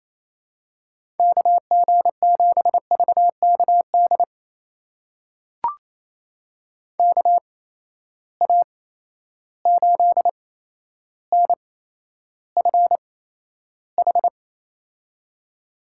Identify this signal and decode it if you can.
Morse code, heard as a keyed tone, about 28 wpm, spaced as for 12 wpm, 700 Hz. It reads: KG7VKD KA8NFH
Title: USA Call Signs - 6 Characters - 8x Character Spacing - 28wpm